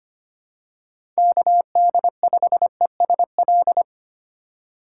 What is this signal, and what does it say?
Morse code, heard as a keyed tone, about 25 wpm, 700 Hz. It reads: KD5ESL
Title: USA Call Signs - 6 Characters - 2x Character Spacing - 25wpm